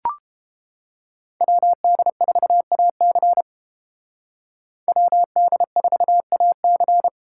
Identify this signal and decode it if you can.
Morse code, heard as a keyed tone, about 29 wpm, 700 Hz. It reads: WD4AC WD4AC